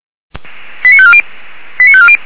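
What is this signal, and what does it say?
Five-tone selective call (ZVEI), heard as quick runs of five tones, tones of about 70 ms. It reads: 89540 89540
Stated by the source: ZVEI five-tone selective call 89248 received by SDR.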